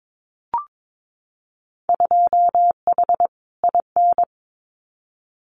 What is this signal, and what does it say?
Morse code, heard as a keyed tone, about 22 wpm, 700 Hz. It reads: 2H IN
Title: Winter Field Day Exchanges Encore 22wpm